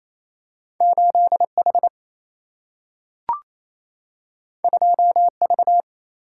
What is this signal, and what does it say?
Morse code, heard as a keyed tone, about 28 wpm, 700 Hz. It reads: 8H 2V
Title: Single-Letter Single-Number 28wpm